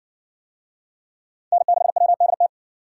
Morse code, heard as a keyed tone, about 60 wpm, 700 Hz. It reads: N6WDT